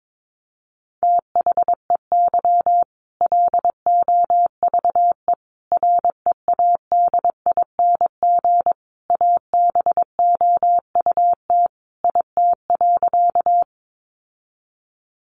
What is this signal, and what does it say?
Morse code, heard as a keyed tone, about 22 wpm, 700 Hz. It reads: THEY LOVE READING ABOUT IT.
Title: Sentences from Top 600 Words 22wpm